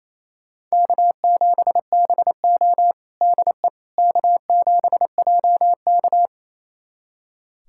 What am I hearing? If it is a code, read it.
Morse code, heard as a keyed tone, about 28 wpm, 700 Hz. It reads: K7BO DE K7JK
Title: USA Callsign DE Callsign - 4 Characters - 28wpm